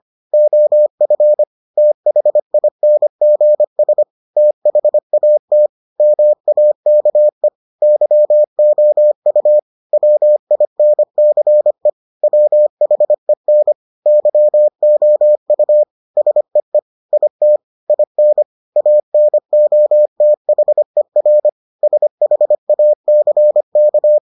Morse code, heard as a keyed tone, about 25 wpm, 600 Hz. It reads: OF THINGS THAT MAKE YOU WINCE WHEN YOU SEE IT IN ANOTHER SHACK